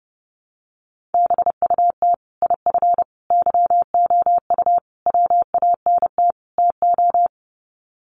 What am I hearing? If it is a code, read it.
Morse code, heard as a keyed tone, about 30 wpm, 700 Hz. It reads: BUT IF YOU WANT TO
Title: Sets of 5 Words Encore2 30wpm